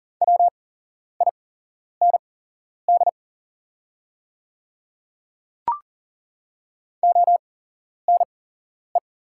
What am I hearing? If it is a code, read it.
Morse code, heard as a keyed tone, about 41 wpm, spaced as for 11 wpm, 690 Hz. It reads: WIND ONE
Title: 8x Character Spacing - Top 500 Words - 40wpm